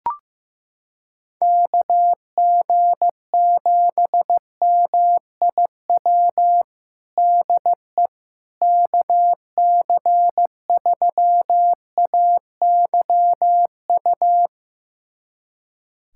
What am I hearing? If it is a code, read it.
Morse code, heard as a keyed tone, about 15 wpm, 700 Hz. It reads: KG7MIW DE KC3AYU
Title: USA Callsign DE Callsign 15wpm